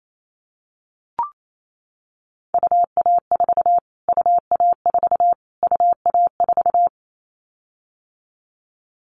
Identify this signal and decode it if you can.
Morse code, heard as a keyed tone, about 28 wpm, 700 Hz. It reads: UA4 UA4 UA4
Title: Callsign Prefixes for Top 25 DXCC - Repeated 3x - 28wpm